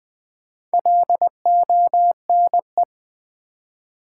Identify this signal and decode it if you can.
Morse code, heard as a keyed tone, about 20 wpm, 700 Hz. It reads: LONE
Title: Build A Word - 2 Letters At A Time - 6-Letter Words - 20wpm